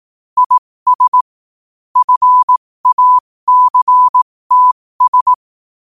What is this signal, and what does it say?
Morse code, heard as a keyed tone, about 18 wpm, spaced as for 15 wpm, 1000 Hz. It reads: IS FACTS
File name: qotdfiles / 15wpm